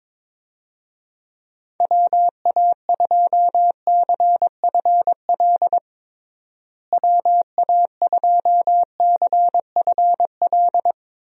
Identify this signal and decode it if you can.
Morse code, heard as a keyed tone, about 22 wpm, 700 Hz. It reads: WA2CFL WA2CFL
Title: USA Call Signs - 6 Characters - Repeated 3x - 22wpm